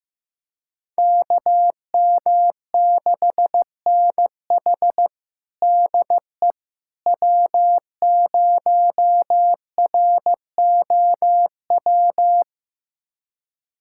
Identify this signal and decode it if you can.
Morse code, heard as a keyed tone, about 15 wpm, 700 Hz. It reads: KM6NH DE W0ROW
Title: USA Callsign DE Callsign - 5 Characters - 15wpm